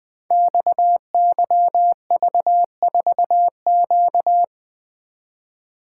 Morse code, heard as a keyed tone, about 20 wpm, 700 Hz. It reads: XYV4Q
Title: Code Groups - 5 Letter:Numbers - 20wpm